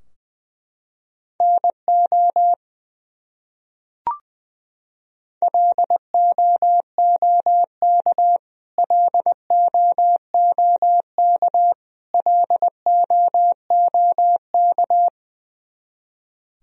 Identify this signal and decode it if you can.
Morse code, heard as a keyed tone, about 21 wpm, 700 Hz. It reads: NO LOOK LOOK LOOK
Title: Top 100 Words Repeated 20wpm